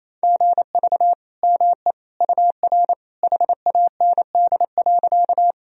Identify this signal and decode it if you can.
Morse code, heard as a keyed tone, about 28 wpm, 700 Hz. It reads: GV ME UR HAND.
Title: Sentences from Top 200 Words and Abbreviations 28wpm